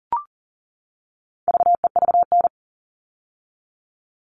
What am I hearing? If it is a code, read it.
Morse code, heard as a keyed tone, about 40 wpm, 700 Hz. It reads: VEVN
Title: All Cut Numbers - 4-Digit 40wpm